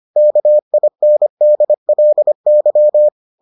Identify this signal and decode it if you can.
Morse code, heard as a keyed tone, about 25 wpm, 600 Hz. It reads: KINDLY